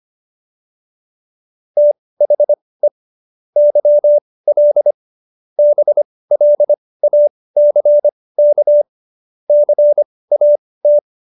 Morse code, heard as a keyed tone, about 25 wpm, spaced as for 18 wpm, 590 Hz. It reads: THE YL BLACK CAT